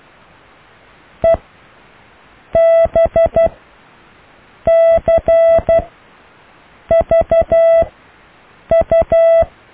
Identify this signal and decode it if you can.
Morse code, heard as a keyed tone, about 12 wpm, spaced as for 6 wpm, 660 Hz. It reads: EBCVU
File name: cw / Alle letters